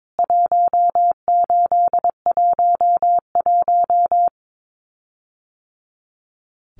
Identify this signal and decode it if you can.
Morse code, heard as a keyed tone, about 22 wpm, 700 Hz. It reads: 1811
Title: CWT Exchange 22wpm